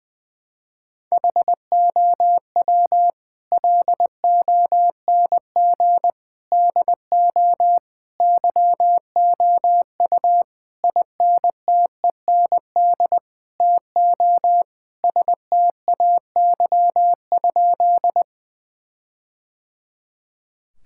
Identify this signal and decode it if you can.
Morse code, heard as a keyed tone, about 20 wpm, 700 Hz. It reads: HOW LONG DO YOU INTEND TO STAY?
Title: Sentences from Top 1600 Words 20wpm